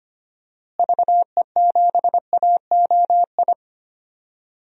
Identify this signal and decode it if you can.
Morse code, heard as a keyed tone, about 25 wpm, 700 Hz. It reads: VE7AOI
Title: DXCC-100 Call Signs - 6 Characters - Repeated 3x - 25wpm